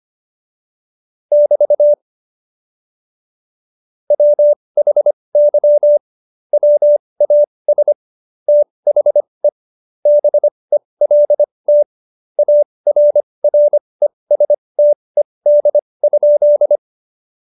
Morse code, heard as a keyed tone, about 25 wpm, spaced as for 20 wpm, 590 Hz. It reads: <BT> WHY WAS THE BELT ARRESTED?
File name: Jokes-25-char-20-eff